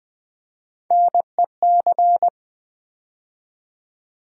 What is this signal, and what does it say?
Morse code, heard as a keyed tone, about 20 wpm, 700 Hz. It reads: NEC
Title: Build A Word - 3 Letters At A Time - 9-Letter Words - 20wpm